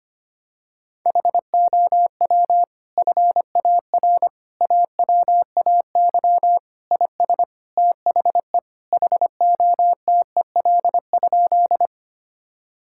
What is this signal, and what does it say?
Morse code, heard as a keyed tone, about 25 wpm, 700 Hz. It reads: HOW FAR AWAY IS THE HOTEL?
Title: Sentences from Top 1600 Words 25wpm